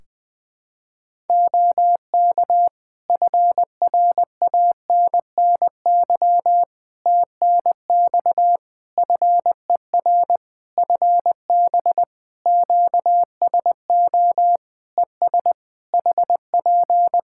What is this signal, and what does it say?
Morse code, heard as a keyed tone, about 19 wpm, 700 Hz. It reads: OK FRANNY TNX FER FB QSO ES HP
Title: Practice QSO - Exchange 4 - Ending - 20wpm